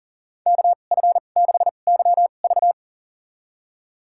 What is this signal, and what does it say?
Morse code, heard as a keyed tone, about 40 wpm, spaced as for 29 wpm, 700 Hz. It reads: KF6YV
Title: USA Call Signs - 2x Character Spacing - 40wpm